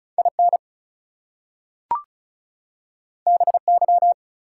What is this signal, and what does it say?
Morse code, heard as a keyed tone, about 36 wpm, 700 Hz. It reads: IN BY
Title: Two Letter Words - 35wpm